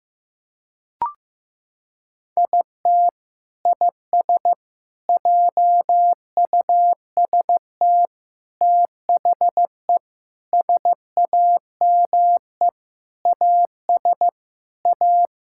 Morse code, heard as a keyed tone, about 15 wpm, 700 Hz. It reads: IT IS JUST THE SAME AS A
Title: Sentences from Top 1000 Words - Review Part 6 - 15wpm